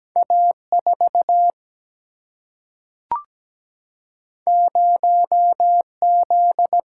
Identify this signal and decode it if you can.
Morse code, heard as a keyed tone, about 17 wpm, 700 Hz. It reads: A 4 0 Z
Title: Single-Letter Single-Number 17wpm